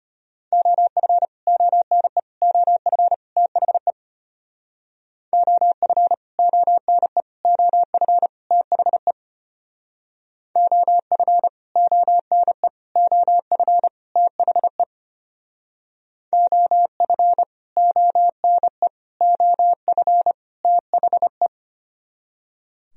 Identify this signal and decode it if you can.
Morse code, heard as a keyed tone, about 31 wpm, 700 Hz. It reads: OF ONE OF THE OF ONE OF THE OF ONE OF THE OF ONE OF THE
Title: Speed-Racing - Sets of 4 Words 25wpm